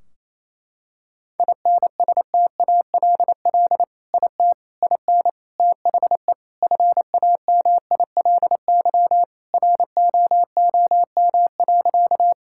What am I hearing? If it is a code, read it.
Morse code, heard as a keyed tone, about 28 wpm, 700 Hz. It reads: INSTALL IT IN THE FAMILY ROOM.
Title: Sentences from Top 1600 Words 28wpm